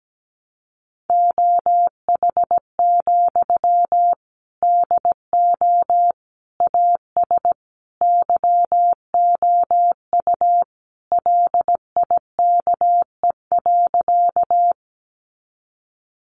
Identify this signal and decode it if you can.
Morse code, heard as a keyed tone, about 17 wpm, 700 Hz. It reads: OH, DO AS YOU LIKE.
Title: Sentences from Top 500 Words 17wpm